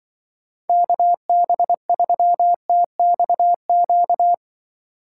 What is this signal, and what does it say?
Morse code, heard as a keyed tone, about 24 wpm, 700 Hz. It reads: KB3TXQ
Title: Speed Racing - USA 2x3 Call Signs 20wpm